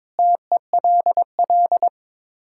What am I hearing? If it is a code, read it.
Morse code, heard as a keyed tone, about 22 wpm, 700 Hz. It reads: TELL
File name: practice / 22wpm / Sets of 6 Words - 22wpm